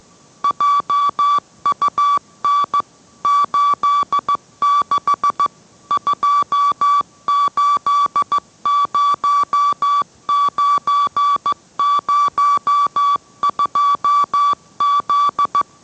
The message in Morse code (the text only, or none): JUN 86 280902Z